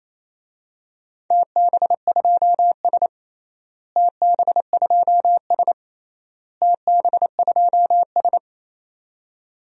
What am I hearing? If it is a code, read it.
Morse code, heard as a keyed tone, about 27 wpm, 700 Hz. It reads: TB2S TB2S TB2S
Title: DXCC-100 Call Signs - 4 Characters - Repeated 3x - 28wpm